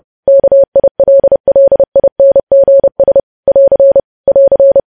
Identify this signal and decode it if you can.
Morse code, heard as a keyed tone, about 30 wpm, 550 Hz. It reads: KILLINGS <AR> <AR>
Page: news headlines in morse code at 30 wpm